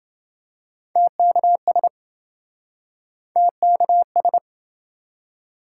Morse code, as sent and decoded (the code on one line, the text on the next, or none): - -.- ... / - -.- ...
TKS TKS